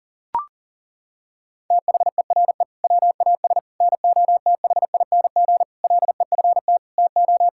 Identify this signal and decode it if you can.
Morse code, heard as a keyed tone, about 37 wpm, 700 Hz. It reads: THERE WAS NOTHING LEFT TO
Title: Sentences from Top 1000 Words - Review Part 6 - 40wpm